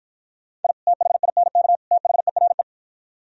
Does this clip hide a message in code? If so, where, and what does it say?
Yes: Morse code from 0.6 to 2.6 s, heard as a keyed tone, about 52 wpm, 700 Hz. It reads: I THINK THERE